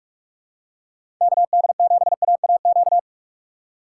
Morse code, heard as a keyed tone, about 46 wpm, 700 Hz. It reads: KD7AAQ
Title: USA Call Signs - 2x Character Spacing - 45wpm